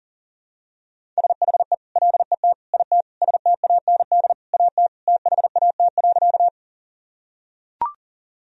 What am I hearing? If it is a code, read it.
Morse code, heard as a keyed tone, about 40 wpm, 700 Hz. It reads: SHE LET IT STAND AT THAT.
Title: Sentences from Top 1000 Words - Review Part 1 - 40wpm